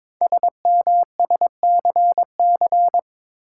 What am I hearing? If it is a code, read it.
Morse code, heard as a keyed tone, about 22 wpm, 700 Hz. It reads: SMSCC
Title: Code Groups - 5 Letters - 22wpm